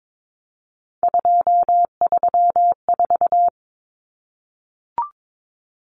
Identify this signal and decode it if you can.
Morse code, heard as a keyed tone, about 22 wpm, 700 Hz. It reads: 234
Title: Numbers - 3-Digit 22wpm